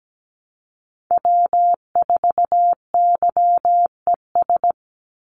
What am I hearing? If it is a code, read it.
Morse code, heard as a keyed tone, about 17 wpm, 700 Hz. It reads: W4YES
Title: Call Signs Repeated 3x - 17wpm